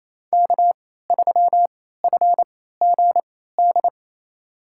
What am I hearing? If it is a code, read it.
Morse code, heard as a keyed tone, about 28 wpm, spaced as for 16 wpm, 700 Hz. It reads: K3FGD
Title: USA Call Signs - 3x Character Spacing - 28wpm